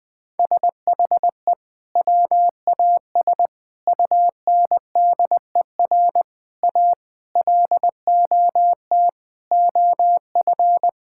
Morse code, heard as a keyed tone, about 20 wpm, 700 Hz. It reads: SHE WAS UNDER A LOT OF
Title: Sentences from Top 1000 Words - Review Part 6 - 20wpm